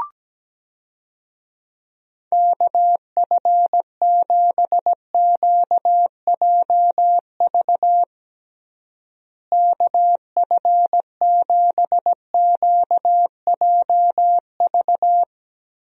Morse code, heard as a keyed tone, about 17 wpm, 700 Hz. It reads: KF7QJV KF7QJV
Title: USA Call Signs - 6 Characters - Repeated 3x - 17wpm